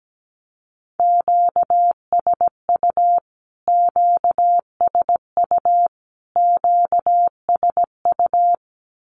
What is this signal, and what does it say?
Morse code, heard as a keyed tone, about 17 wpm, 700 Hz. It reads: QSU QSU QSU
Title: Q-Codes Repeated 3x 17wpm